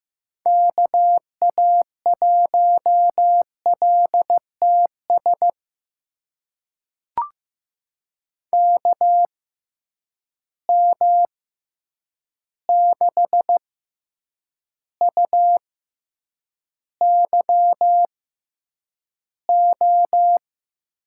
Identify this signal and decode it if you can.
Morse code, heard as a keyed tone, about 15 wpm, spaced as for 8 wpm, 700 Hz. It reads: KA1LTS KM6UYO
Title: USA Call Signs - 6 Characters - 6x Character Spacing - 15wpm